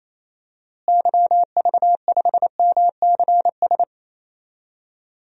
Morse code, heard as a keyed tone, about 28 wpm, 700 Hz. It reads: YV5MCS
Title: DXCC-100 Call Signs - 6 Characters - Repeated 2x - 28wpm